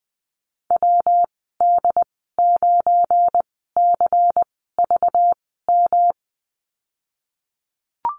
Morse code, heard as a keyed tone, about 20 wpm, spaced as for 15 wpm, 700 Hz. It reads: WD9CVM
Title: 231 Lesson 37g - Letter x - Call Signs 20wpm